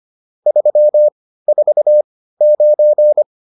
Morse code, heard as a keyed tone, about 25 wpm, spaced as for 15 wpm, 600 Hz. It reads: 349